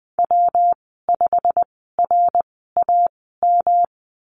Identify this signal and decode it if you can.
Morse code, heard as a keyed tone, about 20 wpm, spaced as for 15 wpm, 700 Hz. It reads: W5RAM
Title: 128 Lesson 21g - Number 6 - Call Signs 20wpm